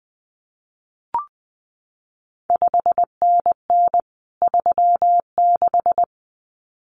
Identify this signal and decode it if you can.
Morse code, heard as a keyed tone, about 20 wpm, 700 Hz. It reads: 5NN 36
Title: CQ WW DX Contest Exchange Encore - 20wpm